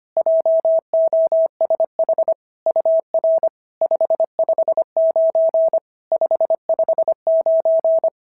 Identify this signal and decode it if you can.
Morse code, heard as a keyed tone, about 25 wpm, 650 Hz. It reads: JOSH UR 559 559
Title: SOTA QSO Practice 25wpm